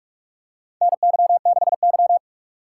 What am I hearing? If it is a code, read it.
Morse code, heard as a keyed tone, about 46 wpm, 700 Hz. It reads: NY6Y
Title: K1USN SST QSO Practice - 45wpm